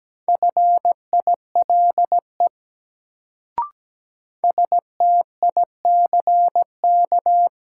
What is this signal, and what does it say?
Morse code, heard as a keyed tone, about 17 wpm, 700 Hz. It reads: FILE STICK
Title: Sentences from Top 1200 Words - New Word Drill Alt - 17wpm